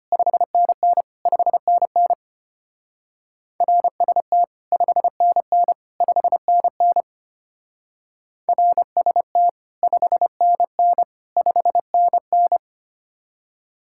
Morse code, heard as a keyed tone, about 33 wpm, 700 Hz. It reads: 5NN 5NN RST 5NN 5NN RST 5NN 5NN